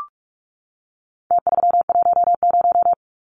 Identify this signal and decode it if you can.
Morse code, heard as a keyed tone, about 44 wpm, 700 Hz. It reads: T310
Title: Numbers - Contest Style - 4-Digit 45wpm